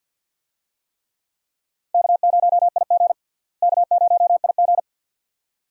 Morse code, heard as a keyed tone, about 51 wpm, 700 Hz. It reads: K0IG K0IG
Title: USA Call Signs - 4 Characters - Repeated 2x - 50wpm